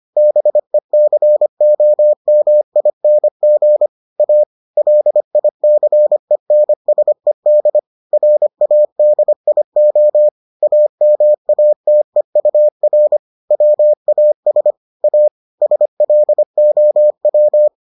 Morse code, heard as a keyed tone, about 25 wpm, 600 Hz. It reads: BECOMING A LICENSED RADIO AMATEUR WAS A SLOW